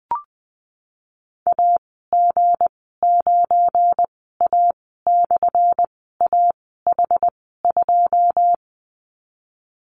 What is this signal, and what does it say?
Morse code, heard as a keyed tone, about 20 wpm, spaced as for 14 wpm, 700 Hz. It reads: AG9A/AH2
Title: 179 Lesson 29g - Slash - Call Signs 20wpm